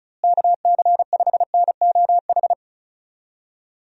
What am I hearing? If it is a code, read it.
Morse code, heard as a keyed tone, about 36 wpm, 700 Hz. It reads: KC5NOH
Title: USA Call Signs - 6 Characters - Repeated 3x - 35wpm